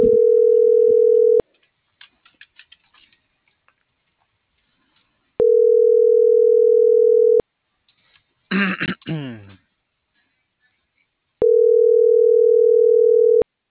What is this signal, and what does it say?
Ringback tone (North America), 440 + 480 Hz.